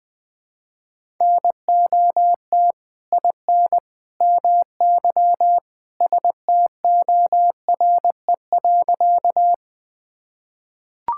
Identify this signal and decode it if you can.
Morse code, heard as a keyed tone, about 20 wpm, 700 Hz. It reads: NOT IN MY STORE.
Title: Sentences from Top 1000 Words - Review Part 5 - 20wpm